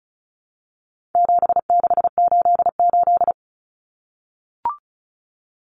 Morse code, heard as a keyed tone, about 34 wpm, 700 Hz. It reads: 7688
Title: Common Cut Numbers - 4-Digit 35wpm